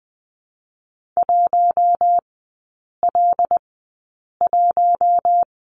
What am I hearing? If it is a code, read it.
Morse code, heard as a keyed tone, about 20 wpm, spaced as for 8 wpm, 700 Hz. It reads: 1L1
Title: 081 Lesson 14e - Number 2 - Three Characters 20wpm